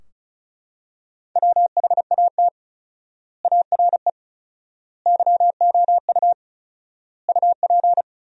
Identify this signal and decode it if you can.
Morse code, heard as a keyed tone, about 34 wpm, 700 Hz. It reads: WHAT ARE YOU UP
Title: 4x Word-Spacing - Sentences from Top 100 Words - 35wpm